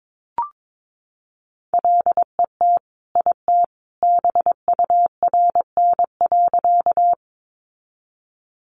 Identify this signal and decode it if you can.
Morse code, heard as a keyed tone, about 22 wpm, 700 Hz. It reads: LET IT BURN.
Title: Sentences from Top 1300 Words 22wpm